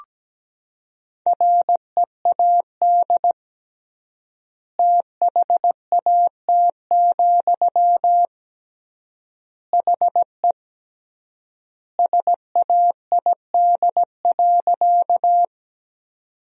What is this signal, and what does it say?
Morse code, heard as a keyed tone, about 17 wpm, 700 Hz. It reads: READ THAT, HE SAID.
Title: 3x Word-Spacing - Sentences from Top 200 Words - 17wpm